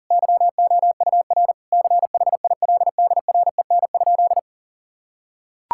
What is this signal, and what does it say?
Morse code, heard as a keyed tone, about 40 wpm, 700 Hz. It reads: YOUR CHILDREN?
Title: Sentences from Top 1000 Words - Review Part 6 - 40wpm